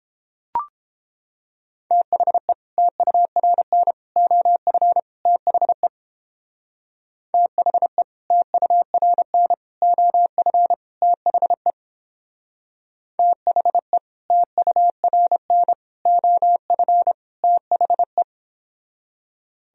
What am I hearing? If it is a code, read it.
Morse code, heard as a keyed tone, about 29 wpm, 700 Hz. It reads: THE TURN OF THE THE TURN OF THE THE TURN OF THE